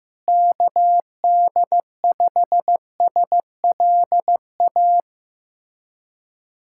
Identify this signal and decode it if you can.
Morse code, heard as a keyed tone, about 15 wpm, 700 Hz. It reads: KD5SLA
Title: USA Callsign DE Callsign - 6 Characters - 15wpm